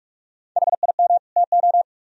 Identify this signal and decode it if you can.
Morse code, heard as a keyed tone, about 45 wpm, 700 Hz. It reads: HIM TO